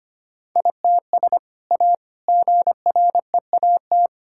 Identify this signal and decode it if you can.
Morse code, heard as a keyed tone, about 25 wpm, 700 Hz. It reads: ITS A GREAT
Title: Sentences from Top 1000 Words - Review Part 6 - 25wpm